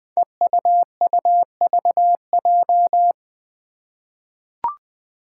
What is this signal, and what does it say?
Morse code, heard as a keyed tone, about 20 wpm, 700 Hz. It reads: EUUVJ
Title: Five Challenging Letter-Number Combinations 20wpm